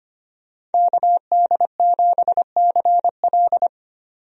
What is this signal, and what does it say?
Morse code, heard as a keyed tone, about 25 wpm, 690 Hz. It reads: KD7CL
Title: USA Call Signs - 5 Characters - 25wpm